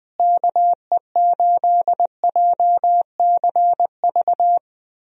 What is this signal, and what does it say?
Morse code, heard as a keyed tone, about 20 wpm, 700 Hz. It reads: KE8JCV
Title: USA Call Signs - 6 Characters - Repeated 2x - 20wpm